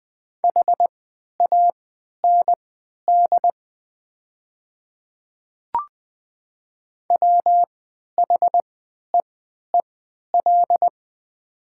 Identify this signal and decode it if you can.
Morse code, heard as a keyed tone, about 20 wpm, spaced as for 11 wpm, 700 Hz. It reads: HAND WHEEL
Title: 3x Character Spacing - Top 500 Words - 20wpm